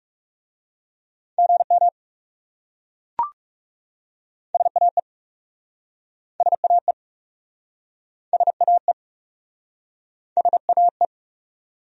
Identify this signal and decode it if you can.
Morse code, heard as a keyed tone, about 40 wpm, 700 Hz. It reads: GM SAE SAE SAE SAE